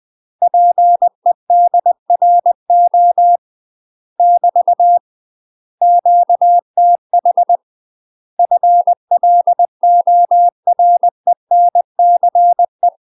Morse code, heard as a keyed tone, about 20 wpm, 700 Hz. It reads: PEDRO <BT> QTH FLORENCE
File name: UK MASTER QSOs TRAINING/20WPM